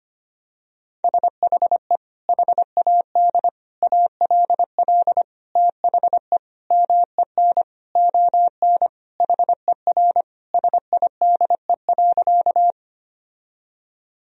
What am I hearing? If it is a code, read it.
Morse code, heard as a keyed tone, about 25 wpm, 700 Hz. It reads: SHE HAD ALL THE MEN ON HER SIDE.
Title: Sentences from Top 1000 Words - Review Part 1 - 25wpm